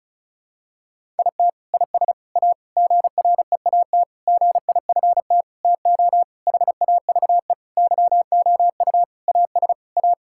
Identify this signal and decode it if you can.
Morse code, heard as a keyed tone, about 36 wpm, 700 Hz. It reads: IT IS A GREAT GIFT TO HAVE YOU AS A
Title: Sentences from Top 1000 Words - Review Part 6 - 35wpm